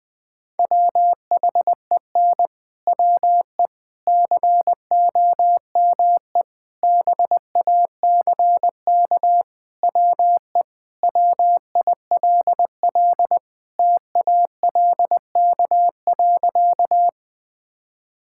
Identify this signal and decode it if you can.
Morse code, heard as a keyed tone, about 21 wpm, 700 Hz. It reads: WHEN WE COME BACK WE WILL TALK.
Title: Sentences with 4-Word Phrases - 20wpm